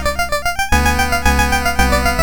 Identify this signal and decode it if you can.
There is no code.